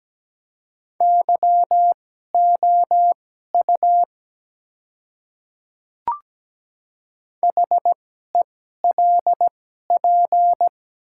Morse code, heard as a keyed tone, about 17 wpm, spaced as for 12 wpm, 700 Hz. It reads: YOU HELP
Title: 2x Character Spacing - Top 200 Words - 17wpm